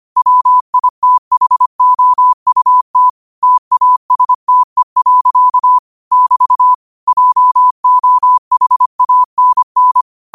Morse code, heard as a keyed tone, about 25 wpm, 1000 Hz. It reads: WITHOUT TASTE. <BT> JOHANN
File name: qotdfiles / 25wpm